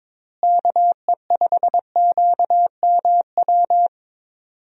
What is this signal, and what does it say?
Morse code, heard as a keyed tone, about 22 wpm, 700 Hz. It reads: KE5QMW
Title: USA Call Signs - 6 Characters - Repeated 3x - 22wpm